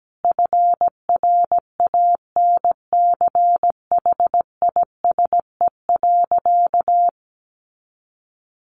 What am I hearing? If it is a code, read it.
Morse code, heard as a keyed tone, about 17 wpm, 700 Hz. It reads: FRANCHISE.
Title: A Sound Of Thunder 17wpm